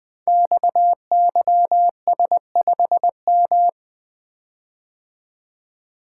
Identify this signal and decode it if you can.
Morse code, heard as a keyed tone, about 20 wpm, 700 Hz. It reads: XYS5M
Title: Code Groups - 5 Letter:Numbers - 20wpm